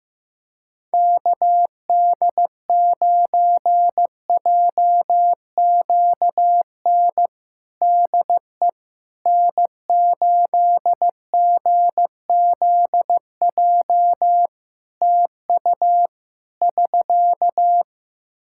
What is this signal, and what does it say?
Morse code, heard as a keyed tone, about 15 wpm, 700 Hz. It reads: KD9JQN DE N8GZJ TU <SK>
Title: K1USN SST QSO Practice 15wpm